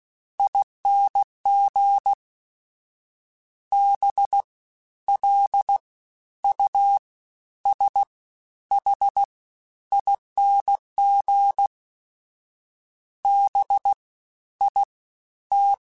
Morse code, heard as a keyed tone, about 16 wpm, spaced as for 11 wpm, 800 Hz. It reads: ING BLUSHING BIT
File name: Special INGs-mp3s / Special INGs-mp3s / Fast ing @15